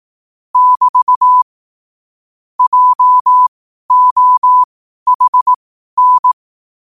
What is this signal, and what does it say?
Morse code, heard as a keyed tone, about 18 wpm, spaced as for 12 wpm, 1000 Hz. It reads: <BT> JOHN
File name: qotdfiles / 12wpm